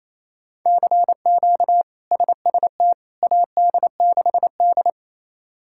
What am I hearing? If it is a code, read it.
Morse code, heard as a keyed tone, about 28 wpm, 700 Hz. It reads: CQ SST AD6D
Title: K1USN SST QSO Practice - 28wpm